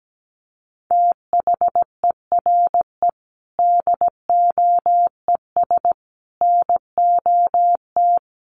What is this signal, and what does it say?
Morse code, heard as a keyed tone, about 17 wpm, 700 Hz. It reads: THERE DOES NOT
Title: Sentences from Top 1400 Words 17wpm